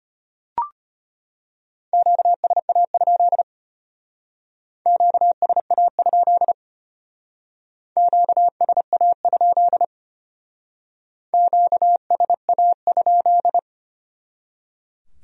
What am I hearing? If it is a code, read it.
Morse code, heard as a keyed tone, about 32 wpm, 700 Hz. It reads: QSA? QSA? QSA? QSA?